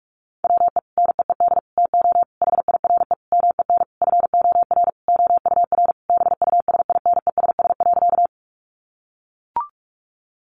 Morse code, heard as a keyed tone, about 44 wpm, 700 Hz. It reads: WE NEED TO HIRE MEN FOR OUR BUSINESS.
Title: Sentences from Top 1400 Words 45wpm